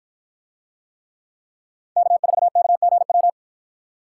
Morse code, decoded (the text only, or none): K4KGW